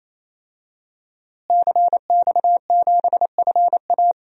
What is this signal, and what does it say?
Morse code, heard as a keyed tone, about 28 wpm, 700 Hz. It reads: CX7FA
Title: DXCC-100 Call Signs - 5 Characters - Repeated 3x - 28wpm